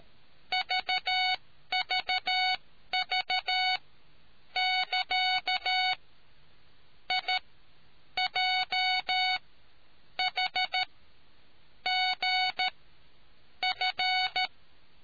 Morse code, heard as a keyed tone, about 13 wpm, spaced as for 8 wpm, 740 Hz. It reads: VVV<KA>IJHGF